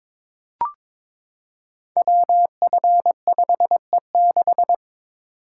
Morse code, heard as a keyed tone, about 22 wpm, 700 Hz. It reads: WF5E6
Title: Code Groups - 5 Letter:Numbers - 22wpm